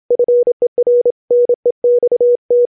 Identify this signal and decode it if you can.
Morse code, heard as a keyed tone, about 27 wpm, 490 Hz. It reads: FER NEXT